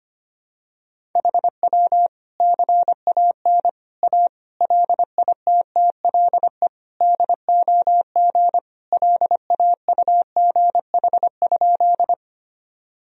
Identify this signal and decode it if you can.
Morse code, heard as a keyed tone, about 25 wpm, 700 Hz. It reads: HW CAN A LITTLE DOG LAUGH?
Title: Sentences from Top 500 Words and Abbreviations 25wpm